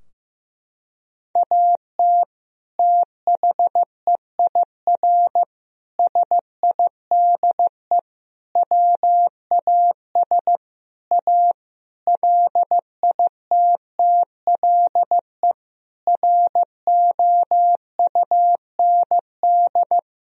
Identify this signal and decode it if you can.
Morse code, heard as a keyed tone, about 15 wpm, 700 Hz. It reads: AT THEIR SIDE WAS A LITTLE ROUND
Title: Sentences from Top 1000 Words - Review Part 2 - 15wpm